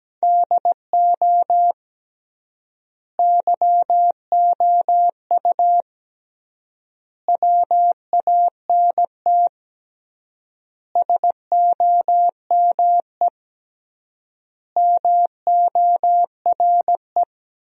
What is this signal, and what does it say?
Morse code, heard as a keyed tone, about 17 wpm, 700 Hz. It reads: DO YOU WANT SOME MORE
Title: 3x Word-Spacing - Sentences from Top 300 Words - 17wpm